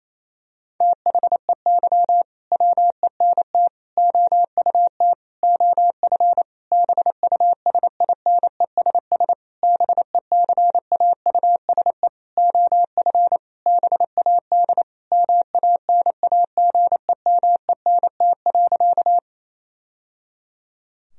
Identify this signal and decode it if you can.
Morse code, heard as a keyed tone, about 28 wpm, 700 Hz. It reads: THEY WENT OUT OF BUSINESS BECAUSE OF BAD MANAGEMENT.
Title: Sentences from Top 1000 Words - Review Part 4 - 28wpm